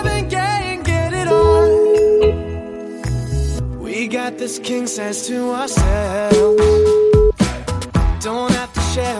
Ringback tone (Europe, 425 Hz), with noise.